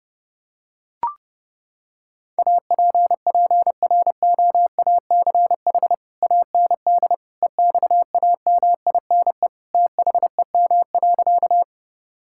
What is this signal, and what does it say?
Morse code, heard as a keyed tone, about 32 wpm, 700 Hz. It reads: APPROACH AND EXAMINE THEM.
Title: Sentences from Top 1600 Words 30wpm